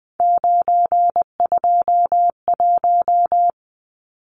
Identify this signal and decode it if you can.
Morse code, heard as a keyed tone, about 19 wpm, 700 Hz. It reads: 921
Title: Numbers - 3-Digit 20wpm